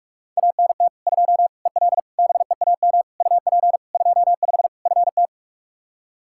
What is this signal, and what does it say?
Morse code, heard as a keyed tone, about 46 wpm, 700 Hz. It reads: ANT 2 EL BEAM UP 25 FT
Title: K1USN SST QSO Practice 45wpm